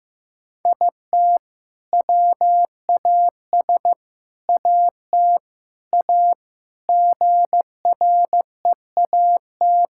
Morse code, heard as a keyed tone, about 15 wpm, 700 Hz. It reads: IT WAS AT A GREAT
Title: Sentences from Top 1600 Words 15wpm